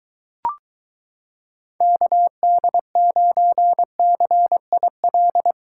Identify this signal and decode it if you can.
Morse code, heard as a keyed tone, about 23 wpm, 700 Hz. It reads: KD9CIL